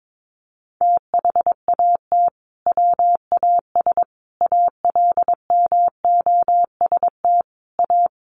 Morse code, heard as a keyed tone, about 22 wpm, 700 Hz. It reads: THAT WAS ALMOST A
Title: Sentences from Top 1400 Words 22wpm